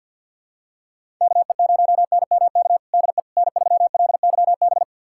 Morse code, heard as a keyed tone, about 52 wpm, 700 Hz. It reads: KE0NMK DE N3LYB
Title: USA Callsign DE Callsign 50wpm